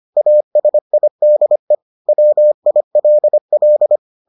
Morse code, heard as a keyed tone, about 25 wpm, 600 Hz. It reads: ASIDE WILL